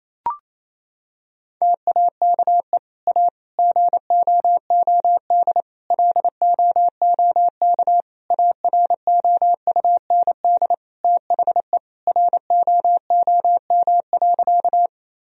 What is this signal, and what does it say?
Morse code, heard as a keyed tone, about 28 wpm, 700 Hz. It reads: TAKE A GOOD LOOK AROUND THE ROOM.
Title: Sentences from Top 1000 Words - Review Part 3 - 28wpm